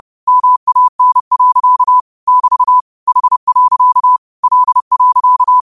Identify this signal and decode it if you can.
Morse code, heard as a keyed tone, about 30 wpm, 1000 Hz. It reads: MAN. <BT> H. L.
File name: qotdfiles / 30wpm